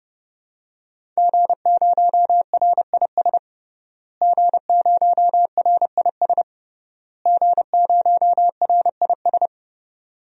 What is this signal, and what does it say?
Morse code, heard as a keyed tone, about 30 wpm, 700 Hz. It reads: G0RIS G0RIS G0RIS
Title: DXCC-100 Call Signs - 5 Characters - Repeated 3x - 30wpm